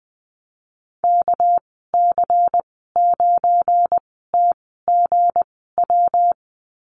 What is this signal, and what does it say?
Morse code, heard as a keyed tone, about 20 wpm, spaced as for 15 wpm, 700 Hz. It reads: KC9TGW
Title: 207 Lesson 33g - Letter j - Call Signs 20wpm